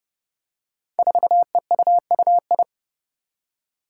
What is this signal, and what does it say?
Morse code, heard as a keyed tone, about 30 wpm, 700 Hz. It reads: 4EUUI
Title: Five Challenging Letter-Number Combinations 30wpm